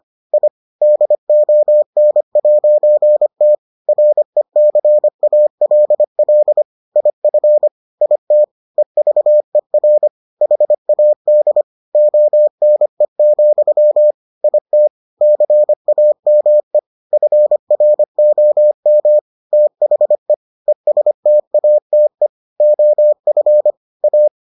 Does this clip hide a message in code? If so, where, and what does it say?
Yes: Morse code from 0.3 to 24.3 s, heard as a keyed tone, about 25 wpm, 600 Hz. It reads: I DON'T RECALL IF IT EVER HAD ONE, IT CAME FROM THE ESTATE OF A